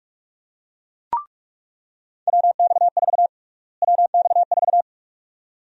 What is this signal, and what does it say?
Morse code, heard as a keyed tone, about 46 wpm, 700 Hz. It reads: WX4 WX4
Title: USA Call Sign Prefixes - Repeated 2x - 45wpm